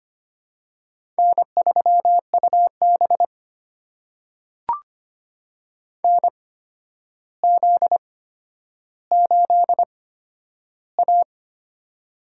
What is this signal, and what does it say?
Morse code, heard as a keyed tone, about 25 wpm, spaced as for 11 wpm, 700 Hz. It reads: N3UB NZ8A
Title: USA Call Signs - 4 Characters - 8x Character Spacing - 25wpm